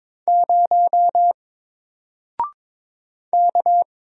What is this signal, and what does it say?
Morse code, heard as a keyed tone, about 22 wpm, 700 Hz. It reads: 0 K
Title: Single Letter-Number Alt 22wpm